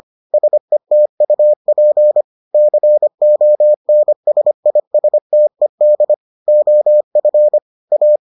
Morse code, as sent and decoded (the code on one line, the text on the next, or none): ... - ..- .--. / -.-. --- -. ... .. ... - . -.. / --- ..-. / .-
SETUP CONSISTED OF A